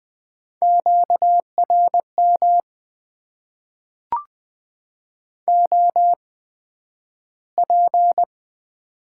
Morse code, heard as a keyed tone, about 20 wpm, spaced as for 11 wpm, 700 Hz. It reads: QRM OP
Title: Top 90 QSO Elements - 8x Character Spacing - 20wpm